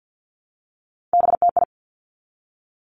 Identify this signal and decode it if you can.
Morse code, heard as a keyed tone, about 50 wpm, 700 Hz. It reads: BTI
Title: Three Challenging Letter-Number Combinations 50wpm